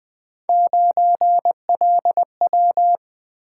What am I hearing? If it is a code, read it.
Morse code, heard as a keyed tone, about 20 wpm, 700 Hz. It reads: 9LW
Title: Three Letter-Number Combinations - 20wpm